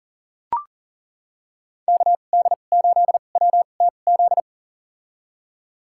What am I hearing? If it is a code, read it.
Morse code, heard as a keyed tone, about 37 wpm, spaced as for 28 wpm, 700 Hz. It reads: KD8WTZ
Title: USA Call Signs - 2x Character Spacing - 40wpm